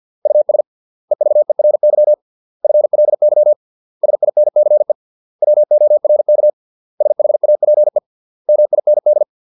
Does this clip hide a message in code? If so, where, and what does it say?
Yes: Morse code from 0.3 to 9.2 s, heard as a keyed tone, about 50 wpm, 600 Hz. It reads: US EVERY FLY SINCE WORK SHAPE MIND